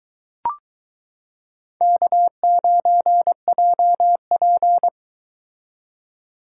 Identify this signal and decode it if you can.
Morse code, heard as a keyed tone, about 22 wpm, 700 Hz. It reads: K9JP